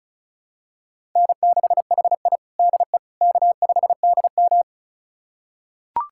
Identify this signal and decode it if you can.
Morse code, heard as a keyed tone, about 36 wpm, 700 Hz. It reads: N6HI DE K5DM
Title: USA Callsign DE Callsign - 4 Characters - 35wpm